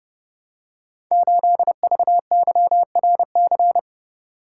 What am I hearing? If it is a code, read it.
Morse code, heard as a keyed tone, about 29 wpm, 700 Hz. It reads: 8VYRC
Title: Code Groups - 5 Letter:Numbers - 30wpm